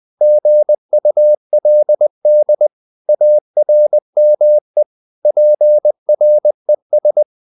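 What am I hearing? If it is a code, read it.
Morse code, heard as a keyed tone, about 20 wpm, 600 Hz. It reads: GULD ARME PRES
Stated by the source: Morse code practice of short Danish words at 20 wpm, from CW med Gnister.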